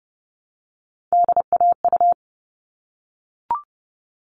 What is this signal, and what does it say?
Morse code, heard as a keyed tone, about 30 wpm, 700 Hz. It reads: DAU
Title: All Cut Numbers - 3-Digit 30wpm